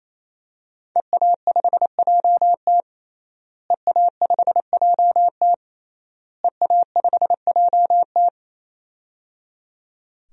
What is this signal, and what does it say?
Morse code, heard as a keyed tone, about 28 wpm, 700 Hz. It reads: EA5JT EA5JT EA5JT
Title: DXCC-100 Call Signs - 5 Characters - Repeated 3x - 28wpm